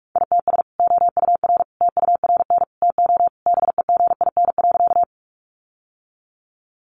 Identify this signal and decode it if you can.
Morse code, heard as a keyed tone, about 44 wpm, 700 Hz. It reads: ITS OUR TURN TO BEGIN.
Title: Sentences from Top 600 Words 45wpm